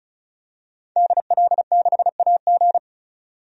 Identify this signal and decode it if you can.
Morse code, heard as a keyed tone, about 36 wpm, 700 Hz. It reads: DL6AG
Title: DXCC-100 Call Signs - 5 Characters - 35wpm